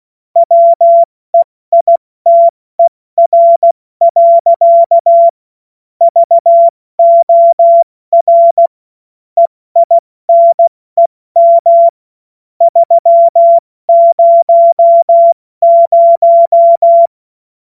Morse code, heard as a keyed tone, about 16 wpm, 680 Hz. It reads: WEITER. VOR EINEM 300
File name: Morsekurs / Witze / Witz 07.November 2023 70 F80